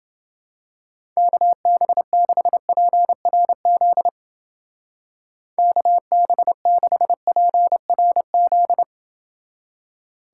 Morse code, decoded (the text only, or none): KB6PRZ KB6PRZ